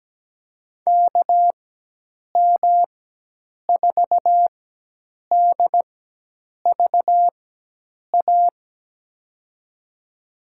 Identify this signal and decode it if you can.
Morse code, heard as a keyed tone, about 17 wpm, spaced as for 8 wpm, 700 Hz. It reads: KM4DVA
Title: USA Call Signs - 6 Characters - 4x Character Spacing - 17wpm